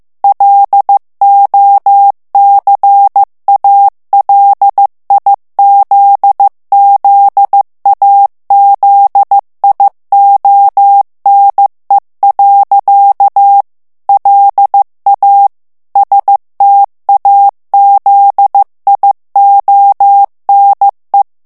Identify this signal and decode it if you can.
Morse code, heard as a keyed tone, about 15 wpm, 800 Hz. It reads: LOCALIZZAZIONE. LA STAZIONE
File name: ITALIEN texte clair 1 vitesse 14 mots minute